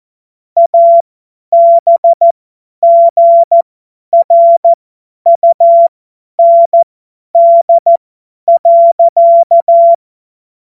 Morse code, heard as a keyed tone, about 14 wpm, spaced as for 10 wpm, 680 Hz. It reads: ABGRUND.
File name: Morsekurs / Witze / Witz 17 Juni 2025 50 F70